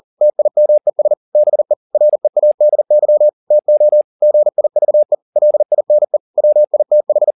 Morse code, heard as a keyed tone, about 40 wpm, 600 Hz. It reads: TIMES BE READY TO GIVE LINE WITH
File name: the old man and the sea 2 40wpm